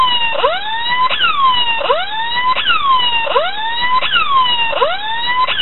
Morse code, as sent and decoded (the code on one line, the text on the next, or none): none